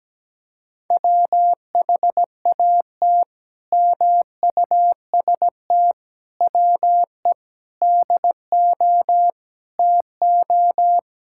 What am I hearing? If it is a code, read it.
Morse code, heard as a keyed tone, about 17 wpm, 700 Hz. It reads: WHAT MUST WE DO TO
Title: Sentences from Top 1000 Words - Review Part 6 - 17wpm